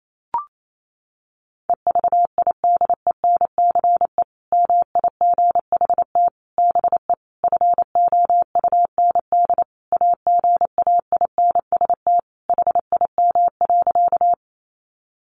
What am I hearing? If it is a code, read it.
Morse code, heard as a keyed tone, about 28 wpm, 700 Hz. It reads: EVIDENCE MIGHT BE FOUND AGAINST HIM.
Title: Sentences from Top 1300 Words 28wpm